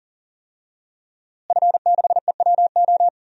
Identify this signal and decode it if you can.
Morse code, heard as a keyed tone, about 40 wpm, 700 Hz. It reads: F6EWO
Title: DXCC-100 Call Signs - 5 Characters - Repeated 3x - 40wpm